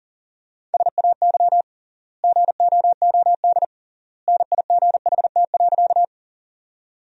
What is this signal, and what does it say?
Morse code, heard as a keyed tone, about 40 wpm, 700 Hz. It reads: SAY GOOD NIGHT.
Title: 3x Word-Spacing - Sentences from Top 300 Words - 40wpm